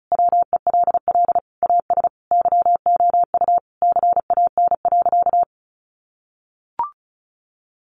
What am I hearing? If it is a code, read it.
Morse code, heard as a keyed tone, about 34 wpm, 700 Hz. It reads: WELL AS YOU CAN.
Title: Sentences from Top 1400 Words 35wpm